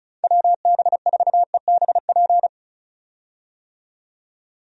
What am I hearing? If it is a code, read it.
Morse code, heard as a keyed tone, about 34 wpm, 700 Hz. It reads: WB4EBP
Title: USA Call Signs - 6 Characters - 35wpm